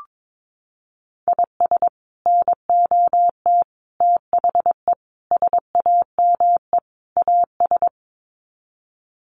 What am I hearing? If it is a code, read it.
Morse code, heard as a keyed tone, about 22 wpm, 700 Hz. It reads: IS NOT THE SAME AS
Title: Sets of 5 Words Encore 22wpm